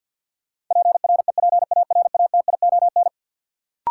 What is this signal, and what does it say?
Morse code, heard as a keyed tone, about 52 wpm, 700 Hz. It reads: PREPARATION
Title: Sentences from Top 1400 Words - New Word Drill Alt - 50wpm